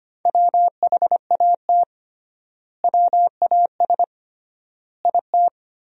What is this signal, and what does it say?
Morse code, heard as a keyed tone, about 25 wpm, 700 Hz. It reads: WHAT WAS IT
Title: 3x Word-Spacing - Sentences from Top 100 Words - 25wpm